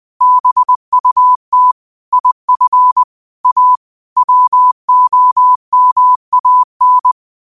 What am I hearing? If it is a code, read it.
Morse code, heard as a keyed tone, about 20 wpm, 1000 Hz. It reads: BUT IF A WOMAN